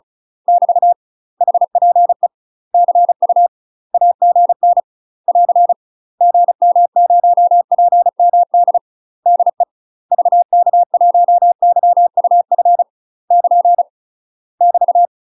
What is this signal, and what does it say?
Morse code, heard as a keyed tone, about 34 wpm, 700 Hz. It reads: <BT> HPE CU AGN <AR> GM0PMD DE VK1YUF <KN> <BT>
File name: UK MASTER QSOs TRAINING/35WPM